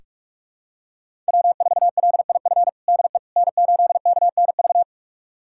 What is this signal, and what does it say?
Morse code, heard as a keyed tone, about 46 wpm, 700 Hz. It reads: W4LIF DE N8KNV
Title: USA Callsign DE Callsign - 5 Characters - 45wpm